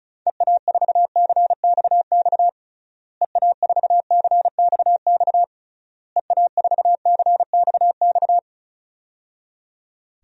Morse code, heard as a keyed tone, about 34 wpm, 700 Hz. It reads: EA4CXX EA4CXX EA4CXX
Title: DXCC-100 Call Signs - 6 Characters - Repeated 3x - 35wpm